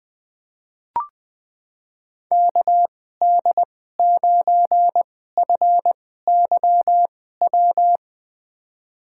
Morse code, heard as a keyed tone, about 20 wpm, spaced as for 14 wpm, 700 Hz. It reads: KD9FYW
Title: USA Call Signs - 6 Characters - 2x Character Spacing - 20wpm